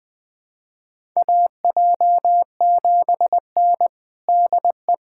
Morse code, heard as a keyed tone, about 21 wpm, 700 Hz. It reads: AJ7N DE
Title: USA Callsign DE Callsign - 4 Characters - 20wpm